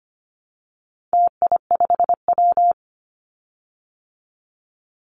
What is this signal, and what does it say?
Morse code, heard as a keyed tone, about 25 wpm, 700 Hz. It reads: TI5W
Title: Call Signs Repeated 2x - 25wpm